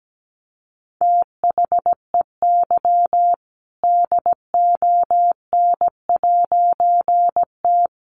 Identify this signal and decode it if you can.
Morse code, heard as a keyed tone, about 17 wpm, 700 Hz. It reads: THEY DON'T
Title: A Sound Of Thunder 17wpm